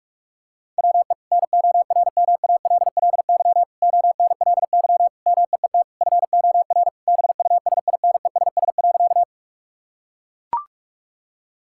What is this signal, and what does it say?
Morse code, heard as a keyed tone, about 42 wpm, 700 Hz. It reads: WE NORMALLY ONLY MEET FOR BUSINESS.
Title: Sentences from Top 1000 Words - Review Part 6 - 45wpm